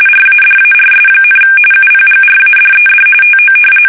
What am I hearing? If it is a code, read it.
RTTY, about 45.45 baud. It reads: RADARS, UPGRADED TO
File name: fsk850 / battleshipnj